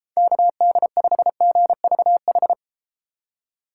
Morse code, heard as a keyed tone, about 32 wpm, 700 Hz. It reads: KD5GVH